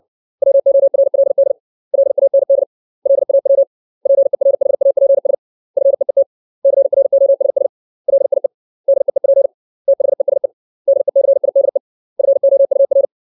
Morse code, heard as a keyed tone, about 60 wpm, 550 Hz. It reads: WORLD LAND LAW PERHAPS FEET CROSS LIE DEEP THESE DECIDE FOUR
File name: 632 words / 632 Words 60WPM